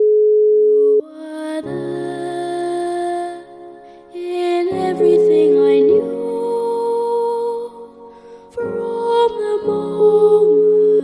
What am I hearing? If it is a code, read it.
Ringback tone (Europe, 425 Hz).